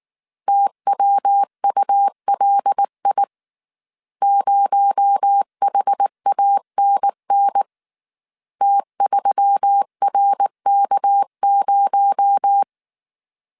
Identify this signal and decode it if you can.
Morse code, heard as a keyed tone, about 19 wpm, 800 Hz. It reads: TWULI 0HANN T3RK0
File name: CW / DET / 1 6 1500C S A 18wpm 2255